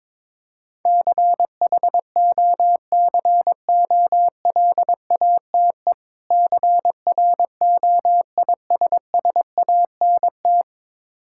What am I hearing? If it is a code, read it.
Morse code, heard as a keyed tone, about 22 wpm, 700 Hz. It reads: CHOCOLATE CROISSANT
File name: Difficult Words with Hints - 22wpm